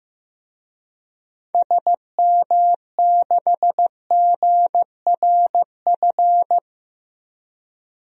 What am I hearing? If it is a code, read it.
Morse code, heard as a keyed tone, about 15 wpm, 700 Hz. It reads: SM6GRF
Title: DXCC-100 Call Signs - 6 Characters - Repeated 2x - 15wpm